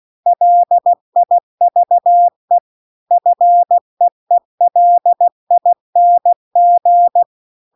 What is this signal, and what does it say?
Morse code, heard as a keyed tone, about 16 wpm, 700 Hz. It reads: LIVE FEELING